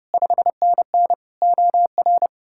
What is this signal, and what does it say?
Morse code, heard as a keyed tone, about 37 wpm, 700 Hz. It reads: 5NN OR
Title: Speed-Racing Common QSO Phrases - 30wpm